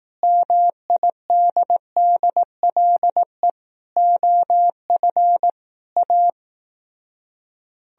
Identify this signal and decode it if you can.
Morse code, heard as a keyed tone, about 18 wpm, 700 Hz. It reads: MIDDLE OF A
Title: Speed-Racing - Sets of 5 Words 15wpm